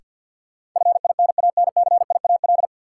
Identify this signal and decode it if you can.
Morse code, heard as a keyed tone, about 52 wpm, 700 Hz. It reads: FINANCIAL
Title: Sentences from Top 1000 Words - Review Part 5 - New Word Drill - 50wpm